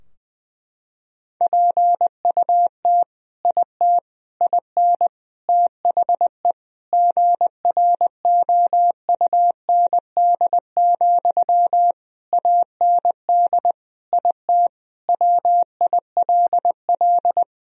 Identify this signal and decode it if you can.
Morse code, heard as a keyed tone, about 19 wpm, 700 Hz. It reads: PUT IT IN THE GROUND, AND IT WILL
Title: Sentences from Top 1000 Words - Review Part 2 - 20wpm